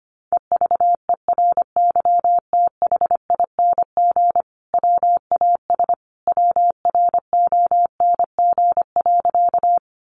Morse code, heard as a keyed tone, about 25 wpm, 700 Hz. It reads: EVERYTHING WAS WRONG.
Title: Sentences from Top 900 Words 25wpm